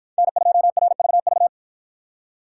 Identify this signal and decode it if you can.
Morse code, heard as a keyed tone, about 51 wpm, 700 Hz. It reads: N2RVV